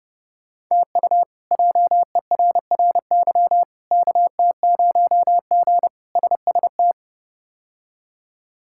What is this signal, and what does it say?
Morse code, heard as a keyed tone, about 30 wpm, 700 Hz. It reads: TU JERRY KT0G SST
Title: K1USN SST QSO Practice - 30wpm